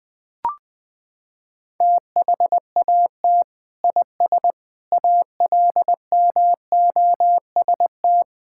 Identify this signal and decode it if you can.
Morse code, heard as a keyed tone, about 21 wpm, 700 Hz. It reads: THAT IS ALMOST
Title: Sentences from Top 1600 Words 20wpm